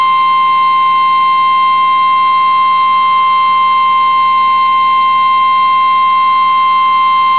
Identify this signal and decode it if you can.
Emergency Alert System (SAME) tones (the 1050 Hz attention tone). It is heard from 0.8 to 7.0 s.
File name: eas received alerts / L1 19557 audio